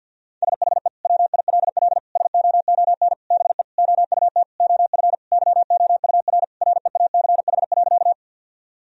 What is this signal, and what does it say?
Morse code, heard as a keyed tone, about 51 wpm, 700 Hz. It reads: SHE WILL SOON BE OUT OF YOUR REACH.
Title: Sentences from Top 1000 Words - Review Part 2 - 50wpm